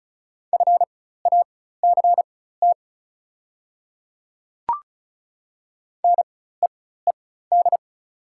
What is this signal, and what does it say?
Morse code, heard as a keyed tone, about 36 wpm, spaced as for 16 wpm, 700 Hz. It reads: FACT NEED
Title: 4x Character Spacing - Top 500 Words - 35wpm